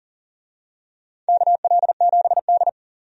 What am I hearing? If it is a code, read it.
Morse code, heard as a keyed tone, about 40 wpm, 700 Hz. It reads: KL7D
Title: DXCC-100 Call Signs - 4 Characters - 40wpm